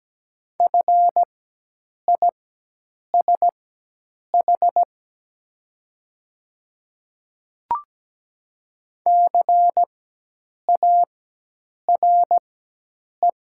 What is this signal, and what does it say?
Morse code, heard as a keyed tone, about 17 wpm, spaced as for 8 wpm, 700 Hz. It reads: FISH CARE
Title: 4x Character Spacing - Top 300 Words - 17wpm